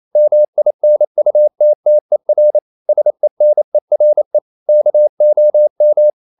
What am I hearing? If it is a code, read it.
Morse code, heard as a keyed tone, about 28 wpm, 600 Hz. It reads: MINUTTER SENERE KOM
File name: Ferie QRQ 28wpm